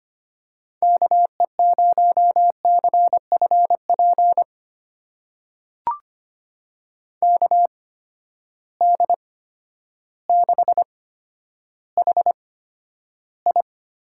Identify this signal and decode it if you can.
Morse code, heard as a keyed tone, about 25 wpm, spaced as for 11 wpm, 700 Hz. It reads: KE0CFP KD6HI
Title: USA Call Signs - 8x Character Spacing - 25wpm